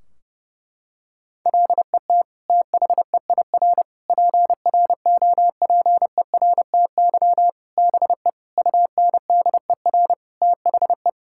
Morse code, heard as a keyed tone, about 30 wpm, 700 Hz. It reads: LET THEIR PROPERTY BE UNDER THE
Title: Sentences from Top 1600 Words 30wpm